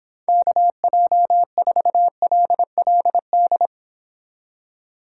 Morse code, heard as a keyed tone, about 26 wpm, 700 Hz. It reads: KJ4LLD